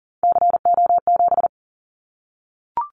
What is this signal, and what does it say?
Morse code, heard as a keyed tone, about 40 wpm, 700 Hz. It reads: CO7
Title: Callsign Prefixes for Top 100 DXCC - 40wpm